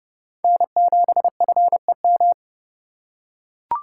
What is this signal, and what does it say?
Morse code, heard as a keyed tone, about 30 wpm, 700 Hz. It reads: N7FEM
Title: USA Call Signs - 5 Characters - 2x Character Spacing - 30wpm